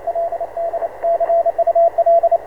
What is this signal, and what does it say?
Morse code, heard as a keyed tone, about 30 wpm, 660 Hz, with radio noise. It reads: LD CUL